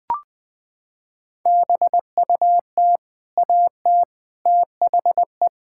Morse code, heard as a keyed tone, about 20 wpm, 700 Hz. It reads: BUT AT THE
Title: Sets of 8 Words - 20wpm